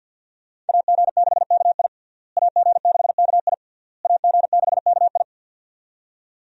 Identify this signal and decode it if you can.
Morse code, heard as a keyed tone, about 52 wpm, 700 Hz. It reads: AG6KI AG6KI AG6KI
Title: USA Call Signs - 5 Characters - Repeated 3x - 50wpm